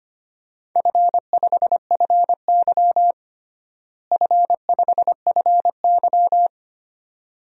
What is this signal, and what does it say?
Morse code, heard as a keyed tone, about 25 wpm, 700 Hz. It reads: F5FY F5FY
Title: DXCC-100 Call Signs - 4 Characters - Repeated 2x - 25wpm